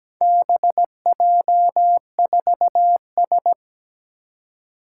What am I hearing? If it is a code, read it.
Morse code, heard as a keyed tone, about 17 wpm, 700 Hz. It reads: BJ4S
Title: Four Challenging Letter-Number Combinations 17wpm